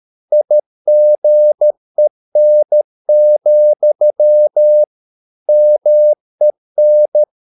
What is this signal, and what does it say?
Morse code, heard as a keyed tone, about 13 wpm, 600 Hz. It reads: IGEN, MEN
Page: Ferie 13 wpm